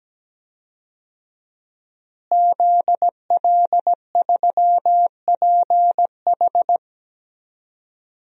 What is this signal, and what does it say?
Morse code, heard as a keyed tone, about 17 wpm, 700 Hz. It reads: ZL3PH
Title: DXCC-100 Call Signs - 5 Characters - Repeated 2x - 17wpm